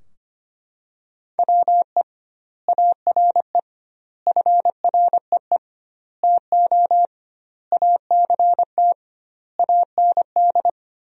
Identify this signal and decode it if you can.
Morse code, heard as a keyed tone, about 25 wpm, 700 Hz. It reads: WE ARE FREE TO ACT AND
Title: 2x Word-Spacing - Sentences from Top 500 Words - 25wpm